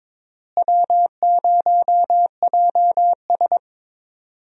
Morse code, heard as a keyed tone, about 22 wpm, 700 Hz. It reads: W0JS
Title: USA Call Signs - 4 Characters - Repeated 3x - 22wpm